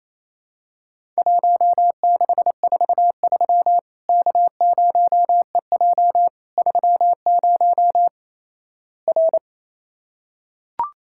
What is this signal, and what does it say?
Morse code, heard as a keyed tone, about 28 wpm, 700 Hz. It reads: 1643 K0EJ 30 R
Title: Worked All Europe DX Contest - QTC Exchange - 28wpm